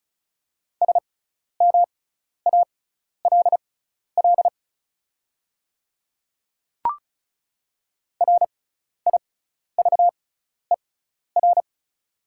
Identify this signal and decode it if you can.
Morse code, heard as a keyed tone, about 36 wpm, spaced as for 12 wpm, 690 Hz. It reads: SMALL RIVER
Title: 6x Character Spacing - Top 300 Words - 35wpm